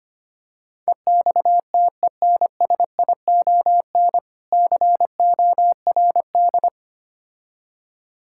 Morse code, heard as a keyed tone, about 25 wpm, 700 Hz. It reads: EXTENSION CORD
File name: practice / 25wpm / Difficult Words with Hints - 25wpm